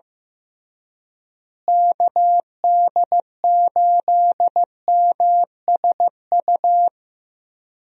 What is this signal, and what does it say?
Morse code, heard as a keyed tone, about 15 wpm, 700 Hz. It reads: KD8MSU
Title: USA Call Signs - 6 Characters - Repeated 3x - 15wpm